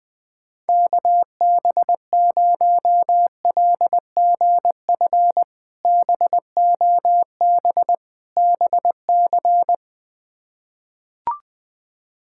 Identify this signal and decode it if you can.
Morse code, heard as a keyed tone, about 20 wpm, 700 Hz. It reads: KB0LGF BOB BC
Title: K1USN SST QSO Practice - 20wpm